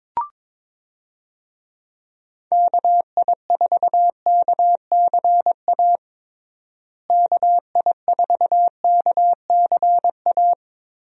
Morse code, heard as a keyed tone, about 22 wpm, 700 Hz. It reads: KI4KCA KI4KCA
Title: USA Call Signs - 6 Characters - Repeated 3x - 22wpm